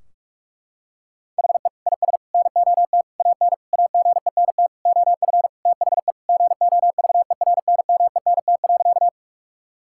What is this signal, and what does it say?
Morse code, heard as a keyed tone, about 46 wpm, 700 Hz. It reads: HE IS NOT AN AGENT OF THE GOVERNMENT.
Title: Sentences from Top 1600 Words 45wpm